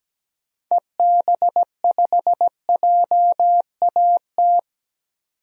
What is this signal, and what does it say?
Morse code, heard as a keyed tone, about 17 wpm, 700 Hz. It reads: EB5JAT
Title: DXCC-100 Call Signs - 6 Characters - 17wpm